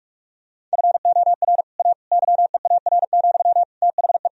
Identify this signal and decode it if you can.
Morse code, heard as a keyed tone, about 43 wpm, 700 Hz. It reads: FOR A YEAR, THE
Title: Sentences from Top 1000 Words - Review Part 1 - 45wpm